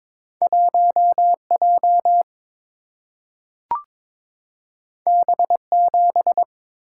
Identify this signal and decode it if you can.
Morse code, heard as a keyed tone, about 22 wpm, 700 Hz. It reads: 1 J B 7
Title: Single-Letter Single-Number 22wpm